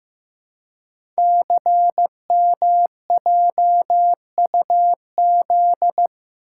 Morse code, heard as a keyed tone, about 15 wpm, 700 Hz. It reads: CMJUZ
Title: Code Groups - 5 Letters - 15wpm